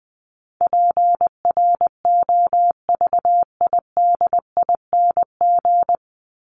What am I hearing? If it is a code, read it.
Morse code, heard as a keyed tone, about 20 wpm, 700 Hz. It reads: PROVIDING
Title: ING Suffix - Top 100 Words with Base Words 20wpm